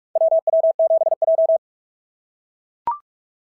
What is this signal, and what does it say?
Morse code, heard as a keyed tone, about 44 wpm, 690 Hz. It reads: WW7J
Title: SOTA QSO Practice 45wpm